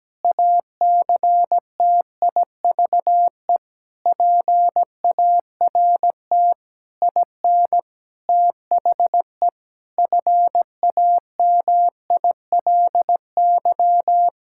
Morse code, heard as a keyed tone, about 17 wpm, 700 Hz. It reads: ACTIVE PART IN THE FAMILY
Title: Sentences from Top 1000 Words - Review Part 6 - 17wpm